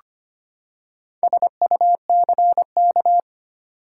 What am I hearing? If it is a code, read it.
Morse code, heard as a keyed tone, about 25 wpm, 700 Hz. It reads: SUCK
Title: Four Letter Words - 25wpm